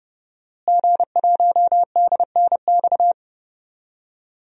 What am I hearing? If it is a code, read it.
Morse code, heard as a keyed tone, about 30 wpm, 700 Hz. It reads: G1DNX
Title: DXCC-100 Call Signs - 5 Characters - Repeated 3x - 30wpm